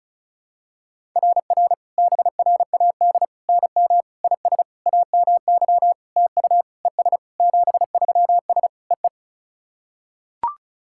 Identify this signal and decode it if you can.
Morse code, heard as a keyed tone, about 34 wpm, 700 Hz. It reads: RR BRAD NM IS AMY TU ES 73S EE
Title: SOTA QSO Practice 35wpm